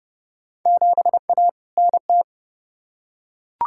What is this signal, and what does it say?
Morse code, heard as a keyed tone, about 32 wpm, 700 Hz. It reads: 7A NT
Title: Field Day Exchanges Encore - 30wpm